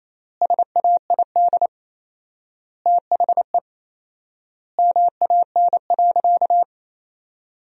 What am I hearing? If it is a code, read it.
Morse code, heard as a keyed tone, about 27 wpm, 700 Hz. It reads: SAID THE MAN.
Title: 4x Word-Spacing - Sentences from Top 200 Words - 28wpm